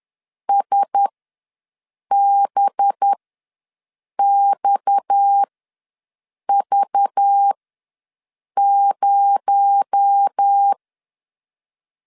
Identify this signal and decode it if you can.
Morse code, heard as a keyed tone, about 10 wpm, spaced as for 6 wpm, 800 Hz. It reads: SBXV0
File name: LEZIONI / 8 YJ45 / 8 YJ ws 10wpm 4043